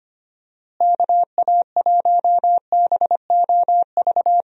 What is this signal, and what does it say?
Morse code, heard as a keyed tone, about 25 wpm, 700 Hz. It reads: KA1BOV
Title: USA Call Signs - 6 Characters - Repeated 3x - 25wpm